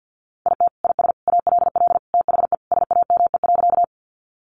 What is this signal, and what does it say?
Morse code, heard as a keyed tone, about 50 wpm, 700 Hz. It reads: IT IS ALL THE SAME.
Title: Sentences from Top 200 Words 50wpm